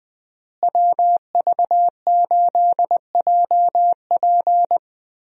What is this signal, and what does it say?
Morse code, heard as a keyed tone, about 21 wpm, 700 Hz. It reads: WV8JP
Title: USA Call Signs - 5 Characters - Repeated 3x - 20wpm